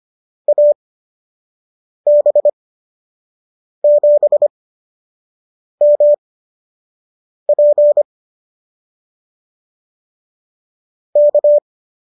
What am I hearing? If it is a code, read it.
Morse code, heard as a keyed tone, about 25 wpm, spaced as for 6 wpm, 590 Hz. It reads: AB7MP K